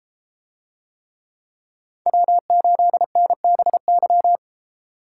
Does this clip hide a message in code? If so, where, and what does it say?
Yes: Morse code from 2.1 to 4.4 s, heard as a keyed tone, about 31 wpm, 700 Hz. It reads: W8NBY